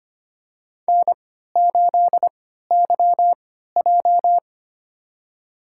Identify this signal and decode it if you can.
Morse code, heard as a keyed tone, about 25 wpm, spaced as for 14 wpm, 700 Hz. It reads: N8YJ
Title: USA Call Signs - 3x Character Spacing - 25wpm